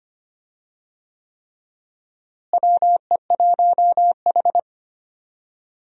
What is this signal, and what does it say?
Morse code, heard as a keyed tone, about 25 wpm, 700 Hz. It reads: WE1H
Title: USA Call Signs - 4 Characters - Repeated 3x - 25wpm